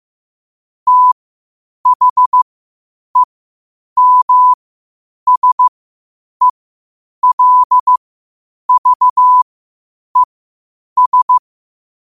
Morse code, heard as a keyed tone, about 15 wpm, spaced as for 8 wpm, 1000 Hz. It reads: THEMSELVES
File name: qotdfiles / 8wpm